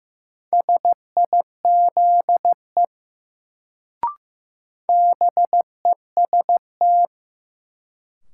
Morse code, heard as a keyed tone, about 15 wpm, 700 Hz. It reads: SIZE BEST
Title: Sentences from Top 400 Words - New Word Drill - 15wpm